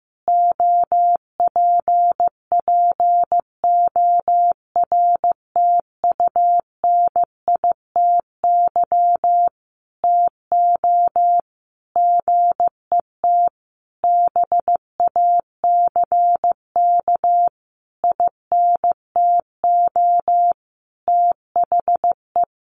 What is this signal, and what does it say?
Morse code, heard as a keyed tone, about 15 wpm, 700 Hz. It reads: OPPORTUNITY TO GET BACK INTO THE
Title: Sentences from Top 1400 Words 15wpm